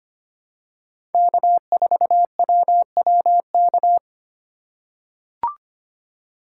Morse code, heard as a keyed tone, about 25 wpm, 700 Hz. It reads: K4WWK
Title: USA Call Signs - 5 Characters - 4x Character Spacing - 25wpm